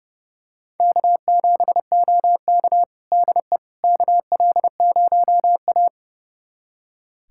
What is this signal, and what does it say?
Morse code, heard as a keyed tone, about 32 wpm, 700 Hz. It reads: K7OK DE KL0A
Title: USA Callsign DE Callsign - 4 Characters - 30wpm